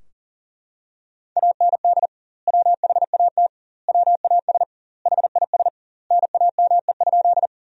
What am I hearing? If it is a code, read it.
Morse code, heard as a keyed tone, about 40 wpm, 700 Hz. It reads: AND WHAT WAS HIS NAME?
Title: 2x Word-Spacing - Sentences from Top 200 Words - 40wpm